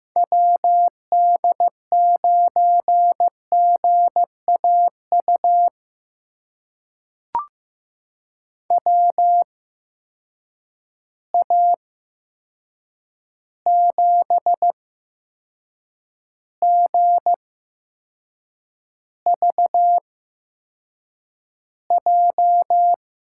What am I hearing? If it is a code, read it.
Morse code, heard as a keyed tone, about 15 wpm, spaced as for 6 wpm, 700 Hz. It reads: WD9GAU WA7GVJ
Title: USA Call Signs - 6 Characters - 8x Character Spacing - 15wpm